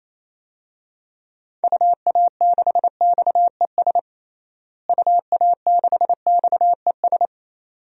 Morse code, heard as a keyed tone, about 27 wpm, 700 Hz. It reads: UA6XES UA6XES
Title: DXCC-100 Call Signs - 6 Characters - Repeated 2x - 28wpm